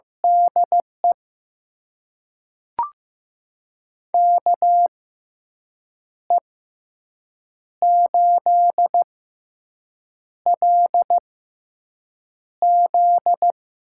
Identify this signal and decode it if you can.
Morse code, heard as a keyed tone, about 15 wpm, spaced as for 6 wpm, 700 Hz. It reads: DE KE8LZ
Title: USA Call Signs - 6 Characters - 6x Character Spacing - 15wpm